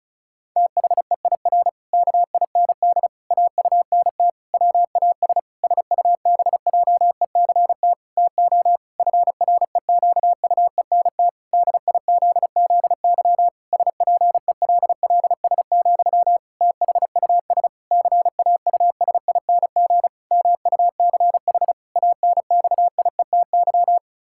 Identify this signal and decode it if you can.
Morse code, heard as a keyed tone, about 36 wpm, 700 Hz. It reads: THEIR KIND AUNT WAS SUBJECT TO FREQUENT DIZZY SPELLS, THUS CAUSING MUCH ANXIETY
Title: Pangram Challenge v2 - 35wpm